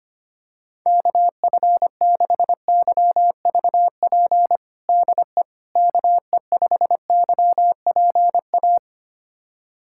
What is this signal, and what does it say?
Morse code, heard as a keyed tone, about 25 wpm, 700 Hz. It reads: KF6YVP DE KE5YPA
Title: USA Callsign DE Callsign 25wpm